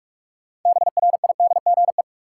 Morse code, heard as a keyed tone, about 46 wpm, 700 Hz. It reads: BRIDGE
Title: Sentences from Top 1200 Words - New Word Drill Alt - 45wpm